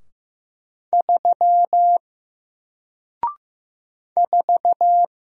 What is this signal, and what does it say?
Morse code, heard as a keyed tone, about 15 wpm, 700 Hz. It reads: 3 4
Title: Single Character (Letter:Number:Punctuation:Prosign) Alt 15wpm